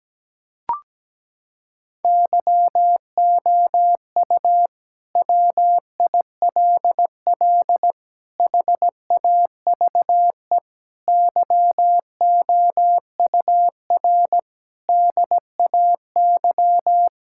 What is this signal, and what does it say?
Morse code, heard as a keyed tone, about 17 wpm, 700 Hz. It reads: YOU WILL HAVE YOUR DAY
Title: Sentences from Top 1600 Words 17wpm